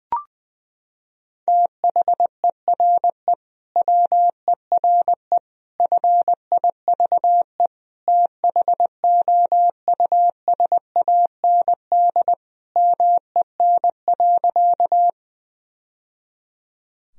Morse code, heard as a keyed tone, about 21 wpm, 700 Hz. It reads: THERE WERE FIVE THOUSAND MEN.
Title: Sentences from Top 1000 Words - Review Part 3 - 20wpm